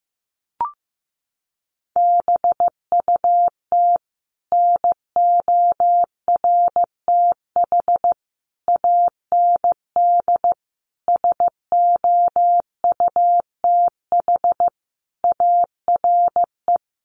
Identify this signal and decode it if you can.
Morse code, heard as a keyed tone, about 15 wpm, 700 Hz. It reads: BUT NORTH AND SOUTH ARE
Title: Sentences from Top 1100 Words 15wpm